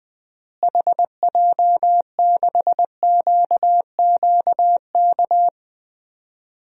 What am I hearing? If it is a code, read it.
Morse code, heard as a keyed tone, about 21 wpm, 700 Hz. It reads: HJ6QQK
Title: DXCC-100 Call Signs - 6 Characters - Repeated 3x - 20wpm